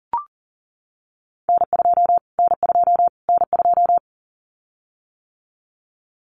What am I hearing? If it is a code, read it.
Morse code, heard as a keyed tone, about 40 wpm, 700 Hz. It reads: N2 N2 N2
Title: Callsign Prefixes for Top 25 DXCC - Repeated 3x - 40wpm